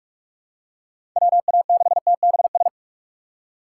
Morse code, heard as a keyed tone, about 46 wpm, 700 Hz. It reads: WA6TBS
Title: USA Call Signs - 2x Character Spacing - 45wpm